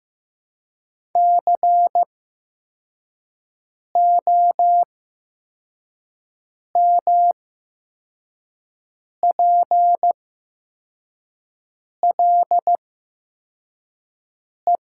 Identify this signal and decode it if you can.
Morse code, heard as a keyed tone, about 15 wpm, spaced as for 4 wpm, 700 Hz. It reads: COMPLE
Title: 8x Character Spacing - Top 500 Words - 15wpm